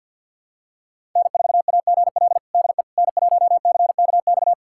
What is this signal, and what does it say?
Morse code, heard as a keyed tone, about 47 wpm, 700 Hz. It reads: N4AGL DE N1CKX
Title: USA Callsign DE Callsign - 5 Characters - 50wpm